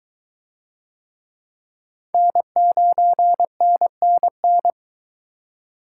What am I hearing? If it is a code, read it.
Morse code, heard as a keyed tone, about 22 wpm, 700 Hz. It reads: N9NNN